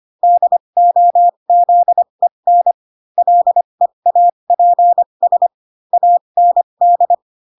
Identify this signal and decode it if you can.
Morse code, heard as a keyed tone, about 25 wpm, spaced as for 22 wpm, 700 Hz. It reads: DOZEN LEAPS AND
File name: Pinocchio / Pinocchio ch6 18 25 ik1hge